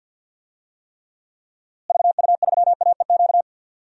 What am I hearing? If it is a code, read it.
Morse code, heard as a keyed tone, about 51 wpm, 700 Hz. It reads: VU3AEQ
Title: DXCC-100 Call Signs - 6 Characters - Repeated 3x - 50wpm